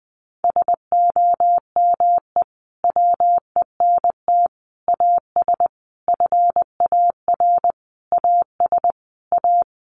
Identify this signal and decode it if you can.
Morse code, heard as a keyed tone, about 20 wpm, 700 Hz. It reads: SOME WENT AS FAR AS A
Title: Sentences from Top 1100 Words 20wpm